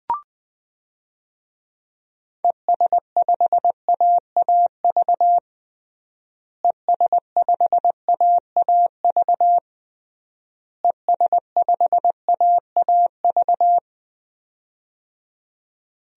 Morse code, heard as a keyed tone, about 20 wpm, 700 Hz. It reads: ES5AAV ES5AAV ES5AAV
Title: DXCC-100 Call Signs - 6 Characters - Repeated 3x - 20wpm